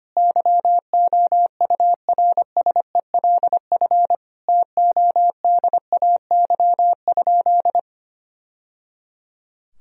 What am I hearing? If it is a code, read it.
Morse code, heard as a keyed tone, about 25 wpm, 700 Hz. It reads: YOURSELF TODAY?
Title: Sentences from Top 1000 Words - Review Part 4 - 25wpm